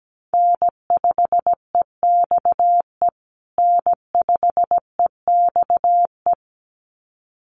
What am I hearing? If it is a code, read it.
Morse code, heard as a keyed tone, about 17 wpm, 700 Hz. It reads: N5EXE N5EXE
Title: USA Call Signs Repeated 2x 17wpm